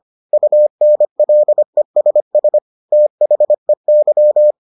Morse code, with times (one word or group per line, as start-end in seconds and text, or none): 0.3-2.6 UNLESS
2.9-4.5 THEY